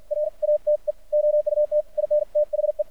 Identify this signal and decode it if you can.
Morse code, heard as a keyed tone, about 53 wpm, spaced as for 46 wpm, 600 Hz. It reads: RATE OUT IN THE
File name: archived / CRSnet